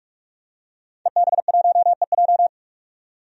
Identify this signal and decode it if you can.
Morse code, heard as a keyed tone, about 41 wpm, 700 Hz. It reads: EB1EJ
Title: DXCC-100 Call Signs - 5 Characters - 45wpm